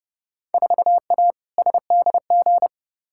Morse code, heard as a keyed tone, about 32 wpm, 700 Hz. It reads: 4A SDG
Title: Field Day Exchanges Encore - 30wpm